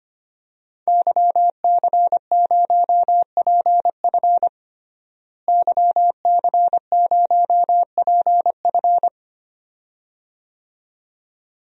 Morse code, heard as a keyed tone, about 25 wpm, 700 Hz. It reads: YC0PF YC0PF
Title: DXCC-100 Call Signs - 5 Characters - Repeated 2x - 25wpm